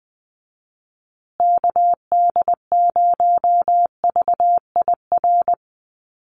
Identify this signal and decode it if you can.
Morse code, heard as a keyed tone, about 20 wpm, 700 Hz. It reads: KD0VIR
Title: USA Call Signs Repeated 3x 20wpm